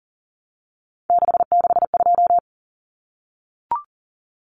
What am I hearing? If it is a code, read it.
Morse code, heard as a keyed tone, about 40 wpm, 700 Hz. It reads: 662
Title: Common Cut Numbers - 3-Digit 40wpm